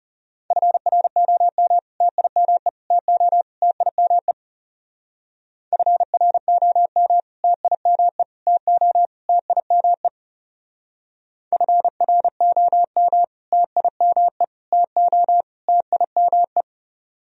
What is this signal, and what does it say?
Morse code, heard as a keyed tone, about 40 wpm, 700 Hz. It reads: FROM TIME TO TIME FROM TIME TO TIME FROM TIME TO TIME